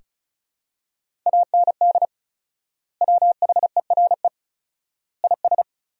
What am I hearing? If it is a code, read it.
Morse code, heard as a keyed tone, about 36 wpm, 700 Hz. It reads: AND WHERE IS
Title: 4x Word-Spacing - Sentences from Top 200 Words - 35wpm